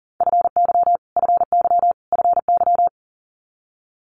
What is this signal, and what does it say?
Morse code, heard as a keyed tone, about 42 wpm, 700 Hz. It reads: FY FY FY
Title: Multi-letter Phonemes Repeated 3x 40wpm